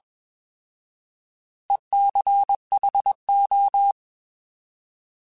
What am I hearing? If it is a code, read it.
Morse code, heard as a keyed tone, about 21 wpm, 800 Hz. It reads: ECHO
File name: Alphabetical / Alphabetical mp3s / Es @20